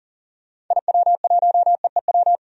Morse code, heard as a keyed tone, about 39 wpm, 700 Hz. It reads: IW1EEW
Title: DXCC-100 Call Signs - 6 Characters - 40wpm